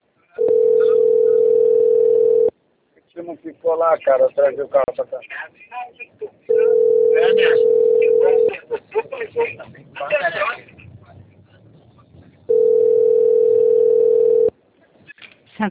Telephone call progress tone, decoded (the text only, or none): ringback tone